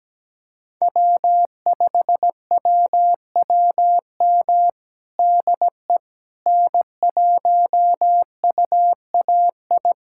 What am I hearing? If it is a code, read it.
Morse code, heard as a keyed tone, about 17 wpm, 700 Hz. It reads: W5WWM DE N1UAI
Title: USA Callsign DE Callsign - 5 Characters - 17wpm